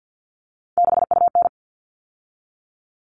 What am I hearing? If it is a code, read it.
Morse code, heard as a keyed tone, about 50 wpm, 700 Hz. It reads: 6UN
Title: All Cut Numbers - 3-Digit 50wpm